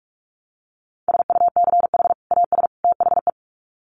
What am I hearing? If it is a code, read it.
Morse code, heard as a keyed tone, about 44 wpm, 700 Hz. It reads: SUCH AS THE
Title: Sets of 3 Words 45wpm